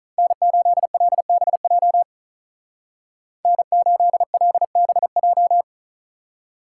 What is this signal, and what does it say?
Morse code, heard as a keyed tone, about 43 wpm, 700 Hz. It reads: N8LBJ N8LBJ
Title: Speed Racing - USA 1x3 Call Signs 35wpm